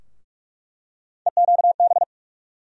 Morse code, heard as a keyed tone, about 46 wpm, 700 Hz. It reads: EQB
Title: USA Call Sign Postfixes - 3 Letters 45wpm